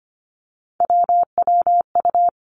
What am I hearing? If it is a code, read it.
Morse code, heard as a keyed tone, about 25 wpm, 700 Hz. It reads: WWU
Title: Three Challenging Letter-Number Combinations 25wpm